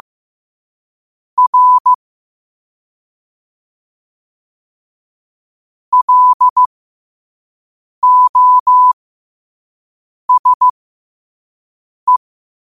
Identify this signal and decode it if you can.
Morse code, heard as a keyed tone, about 15 wpm, spaced as for 5 wpm, 1000 Hz. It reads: R LOSE
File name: qotdfiles / 5wpm